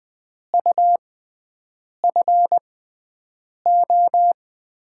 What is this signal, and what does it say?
Morse code, heard as a keyed tone, about 20 wpm, spaced as for 7 wpm, 700 Hz. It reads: UFO
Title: USA Call Signs - 6x Character Spacing - 20wpm